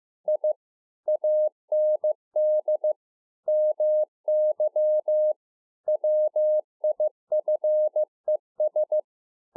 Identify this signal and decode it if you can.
Morse code, heard as a keyed tone, about 15 wpm, 610 Hz. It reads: I AND MY WIFES